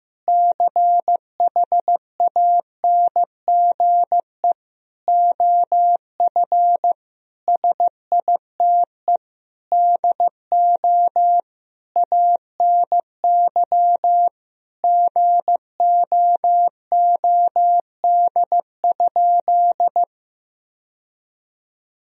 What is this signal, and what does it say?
Morse code, heard as a keyed tone, about 16 wpm, 700 Hz. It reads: CHANGE OF SITE DO ANY GOOD?
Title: Sentences from Top 1000 Words - Review Part 4 - 15wpm